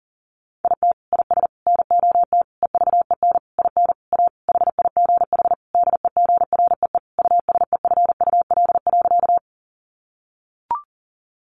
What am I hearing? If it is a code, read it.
Morse code, heard as a keyed tone, about 40 wpm, 700 Hz. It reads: IT IS NOT EVEN IN A HIGH DEGREE USEFUL.
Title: Sentences from Top 1500 Words 40wpm